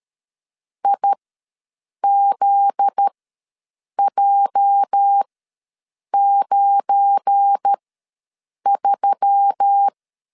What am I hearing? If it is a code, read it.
Morse code, heard as a keyed tone, about 13 wpm, spaced as for 7 wpm, 800 Hz. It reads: IZJ93
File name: LEZIONI / 8 YJ45 / 8 YJ ws 12wpm 3342